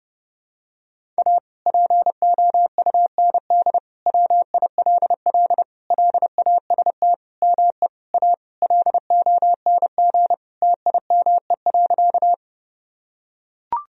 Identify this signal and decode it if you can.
Morse code, heard as a keyed tone, about 32 wpm, 700 Hz. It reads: A POUND WILL LAST ME A LONG TIME.
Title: Sentences from Top 1000 Words - Review Part 2 - 30wpm